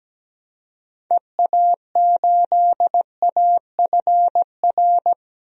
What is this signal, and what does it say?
Morse code, heard as a keyed tone, about 17 wpm, 700 Hz. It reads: EA8AFR
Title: DXCC-100 Call Signs - 6 Characters - 17wpm